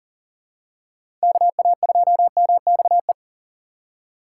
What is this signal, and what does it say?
Morse code, heard as a keyed tone, about 40 wpm, 700 Hz. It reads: KA2MXE
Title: USA Call Signs - 6 Characters - 2x Character Spacing - 40wpm